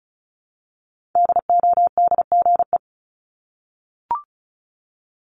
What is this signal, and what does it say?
Morse code, heard as a keyed tone, about 34 wpm, 700 Hz. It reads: DODGE
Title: Multi-letter Phonemes Repeated 3x 35wpm